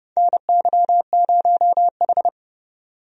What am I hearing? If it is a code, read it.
Morse code, heard as a keyed tone, about 31 wpm, 700 Hz. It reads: NY0H
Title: USA Call Signs - 4 Characters - 30wpm